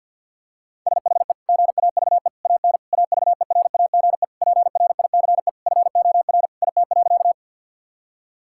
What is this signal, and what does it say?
Morse code, heard as a keyed tone, about 50 wpm, 700 Hz. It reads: SHE GAVE AN AVERAGE PRICE FOR IT.
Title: Sentences from Top 1000 Words - Review Part 6 - 50wpm